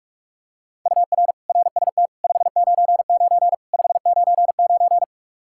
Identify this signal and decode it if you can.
Morse code, heard as a keyed tone, about 46 wpm, 700 Hz. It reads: UR RST 599 599
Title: Practice QSO - Exchange 1 - Beginning - 45wpm